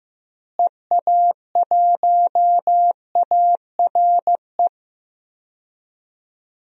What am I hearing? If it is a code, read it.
Morse code, heard as a keyed tone, about 15 wpm, 700 Hz. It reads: EA1ARE
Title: DXCC-100 Call Signs - 6 Characters - Repeated 2x - 15wpm